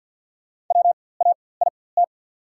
Morse code, heard as a keyed tone, about 54 wpm, spaced as for 24 wpm, 690 Hz. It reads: WAIT
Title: 4x Character Spacing - Top 500 Words - 50wpm